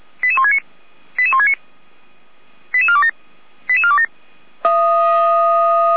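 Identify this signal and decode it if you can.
Five-tone selective call (ZVEI), heard as quick runs of five tones, tones of about 70 ms. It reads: 88169 88169 88427 88427
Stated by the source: ZVEI five-tone selective call on 455 received by SDR, logged by Feuerwehr Orlamünde.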